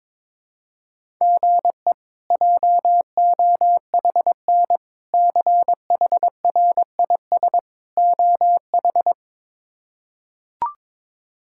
Morse code, heard as a keyed tone, about 22 wpm, 700 Hz. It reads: GE JOHN CHRIS OH
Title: K1USN SST QSO Practice - 22wpm